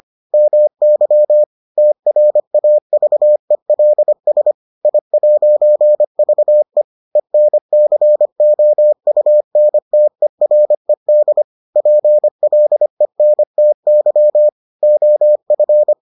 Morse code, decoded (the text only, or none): MY TRAVELS I'VE ENCOUNTERED PLENTY OF